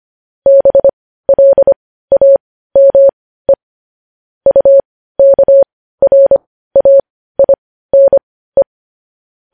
Morse code, heard as a keyed tone, about 25 wpm, spaced as for 15 wpm, 550 Hz. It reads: BLAME UKRAINE
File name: podcast / News headlines-1x-25x15